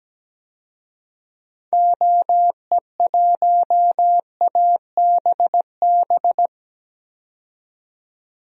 Morse code, heard as a keyed tone, about 17 wpm, 700 Hz. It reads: OE1ABB
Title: DXCC-100 Call Signs - 6 Characters - 17wpm